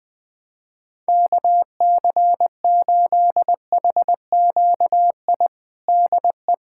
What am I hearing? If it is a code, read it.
Morse code, heard as a keyed tone, about 20 wpm, 700 Hz. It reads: KC8HQI DE
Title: USA Callsign DE Callsign 20wpm